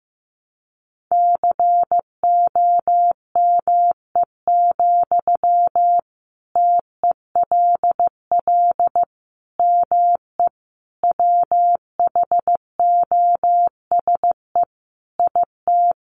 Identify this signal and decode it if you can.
Morse code, heard as a keyed tone, about 15 wpm, 700 Hz. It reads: COME, TELL ME WHOSE IT
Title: Sentences from Top 1200 Words 15wpm